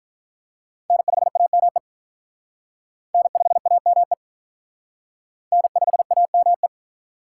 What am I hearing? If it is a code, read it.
Morse code, heard as a keyed tone, about 44 wpm, 700 Hz. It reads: N5AME N5AME N5AME